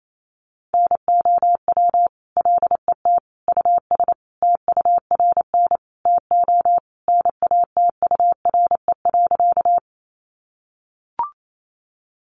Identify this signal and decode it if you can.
Morse code, heard as a keyed tone, about 28 wpm, 700 Hz. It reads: NOW LET US TURN TO NATURE.
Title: Sentences from Top 600 Words 28wpm